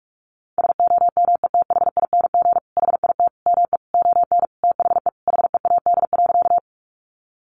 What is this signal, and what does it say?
Morse code, heard as a keyed tone, about 44 wpm, 700 Hz. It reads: SOMETHING HIT ME ON THE HEAD.
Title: Sentences from Top 1200 Words 45wpm